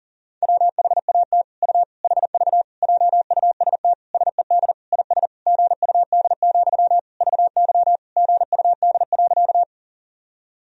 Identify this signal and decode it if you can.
Morse code, heard as a keyed tone, about 40 wpm, 700 Hz. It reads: WHAT U HV JUST SED IS GUD, VY GUD.
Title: Sentences from Top 200 Words and Abbreviations 40wpm